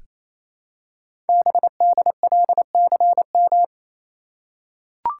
Morse code, heard as a keyed tone, about 29 wpm, 700 Hz. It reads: BDLCM
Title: Code Groups - 5 Letters - 28wpm